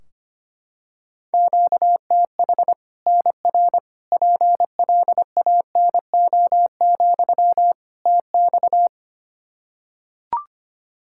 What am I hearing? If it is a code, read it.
Morse code, heard as a keyed tone, about 25 wpm, 700 Hz. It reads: QTH NR PLANO, TX
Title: Practice QSO - Exchange 1 - Beginning - 25wpm